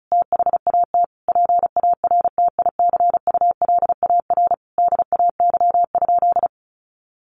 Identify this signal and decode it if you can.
Morse code, heard as a keyed tone, about 34 wpm, 700 Hz. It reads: THAT PARTICULAR DAY?
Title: Sentences from Top 1400 Words 35wpm